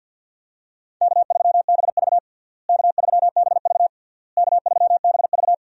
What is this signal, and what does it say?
Morse code, heard as a keyed tone, about 52 wpm, 700 Hz. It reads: K3BV K3BV K3BV
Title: USA Call Signs - 4 Characters - Repeated 3x - 50wpm